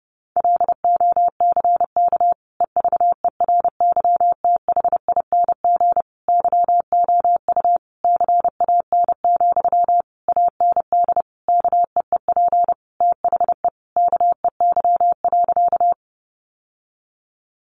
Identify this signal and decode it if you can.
Morse code, heard as a keyed tone, about 30 wpm, 700 Hz. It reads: LOCK EVERYTHING YOU CAN, AND KEEP THE KEY.
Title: Sentences from Top 1300 Words 30wpm